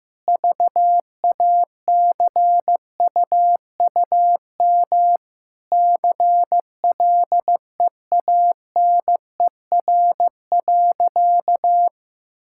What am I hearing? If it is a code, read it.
Morse code, heard as a keyed tone, about 16 wpm, 700 Hz. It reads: VACUUM CLEANER.
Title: Pangram Challenge v2 - 15wpm